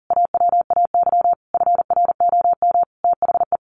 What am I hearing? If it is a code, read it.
Morse code, heard as a keyed tone, about 40 wpm, 700 Hz. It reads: AWAY FROM THE
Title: A Sound Of Thunder 40wpm